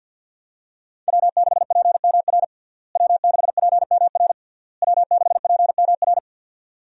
Morse code, heard as a keyed tone, about 51 wpm, 700 Hz. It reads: W6PMR W6PMR W6PMR
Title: USA Call Signs - 5 Characters - Repeated 3x - 50wpm